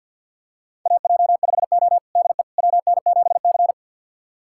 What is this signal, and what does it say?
Morse code, heard as a keyed tone, about 53 wpm, 700 Hz. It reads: AJ5O DE WN7C
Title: USA Callsign DE Callsign - 4 Characters - 50wpm